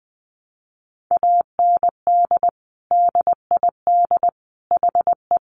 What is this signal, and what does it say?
Morse code, heard as a keyed tone, about 20 wpm, 700 Hz. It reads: AND DID HE
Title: Sentences from Top 1100 Words 20wpm